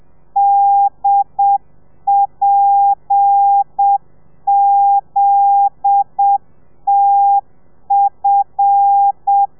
Morse code, heard as a keyed tone, about 7 wpm, 800 Hz. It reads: DPZTF